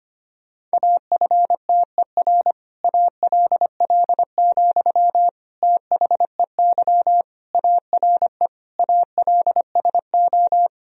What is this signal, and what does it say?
Morse code, heard as a keyed tone, about 25 wpm, 700 Hz. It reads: AFTER ALL, THEY ARE ALSO
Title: Sentences from Top 1000 Words - Review Part 1 - 25wpm